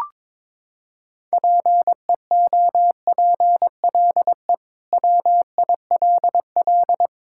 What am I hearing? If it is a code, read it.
Morse code, heard as a keyed tone, about 22 wpm, 700 Hz. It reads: PEOPLE WILL
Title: Sentences from Top 1000 Words - Review Part 6 - 22wpm